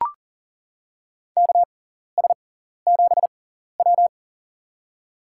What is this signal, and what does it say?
Morse code, heard as a keyed tone, about 38 wpm, spaced as for 14 wpm, 700 Hz. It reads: KS7W
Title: USA Call Signs - 6x Character Spacing - 40wpm